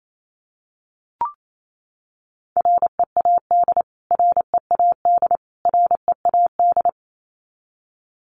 Morse code, heard as a keyed tone, about 28 wpm, 700 Hz. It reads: READ READ READ
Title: Top 500 Words Repeated 28wpm